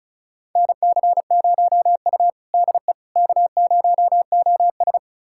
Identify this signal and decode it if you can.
Morse code, heard as a keyed tone, about 36 wpm, 700 Hz. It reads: NC0U DE K0OS
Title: USA Callsign DE Callsign - 4 Characters - 35wpm